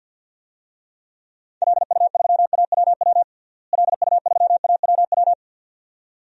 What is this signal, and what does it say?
Morse code, heard as a keyed tone, about 53 wpm, 700 Hz. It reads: LU3AWW LU3AWW
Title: DXCC-100 Call Signs - 6 Characters - Repeated 2x - 50wpm